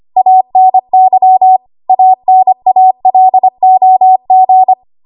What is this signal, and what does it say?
Morse code, heard as a keyed tone, about 25 wpm, 750 Hz. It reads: ANY ANALOG
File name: Archive / 25 WPM / 25WPM